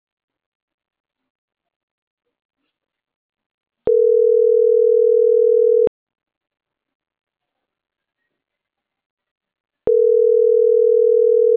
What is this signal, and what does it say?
Ringback tone (North America), 440 + 480 Hz.